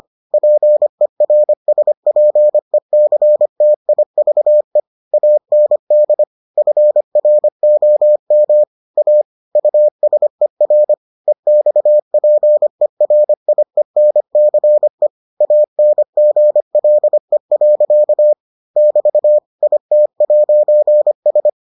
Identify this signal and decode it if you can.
Morse code, heard as a keyed tone, about 25 wpm, 600 Hz. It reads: PERSPECTIVE AND FROM A USER EXPERIENCE ANGLE. <BT> IT'S